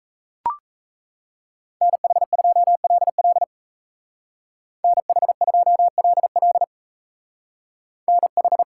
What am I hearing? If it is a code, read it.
Morse code, heard as a keyed tone, about 40 wpm, 700 Hz. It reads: NH2LL NH2LL NH